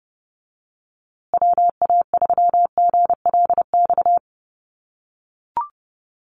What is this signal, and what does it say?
Morse code, heard as a keyed tone, about 30 wpm, 700 Hz. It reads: WA3GLX
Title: USA Call Signs Repeated 2x 30wpm